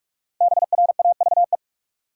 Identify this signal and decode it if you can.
Morse code, heard as a keyed tone, about 46 wpm, 700 Hz. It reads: BRAVE
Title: Five Letter Words - 45wpm